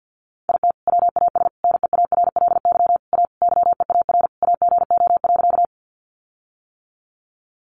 Morse code, heard as a keyed tone, about 50 wpm, 700 Hz. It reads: IT WAS NEARLY A YEAR AGO.
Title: Sentences from Top 1300 Words 50wpm